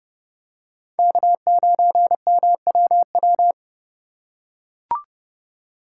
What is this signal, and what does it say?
Morse code, heard as a keyed tone, about 28 wpm, 700 Hz. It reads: K9MWW